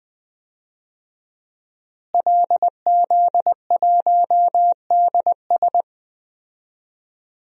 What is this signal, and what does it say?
Morse code, heard as a keyed tone, about 20 wpm, 700 Hz. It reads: LZ1DS
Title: DXCC-100 Call Signs - 5 Characters - Repeated 2x - 20wpm